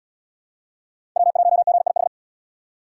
Morse code, heard as a keyed tone, about 68 wpm, 700 Hz. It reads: W1GIH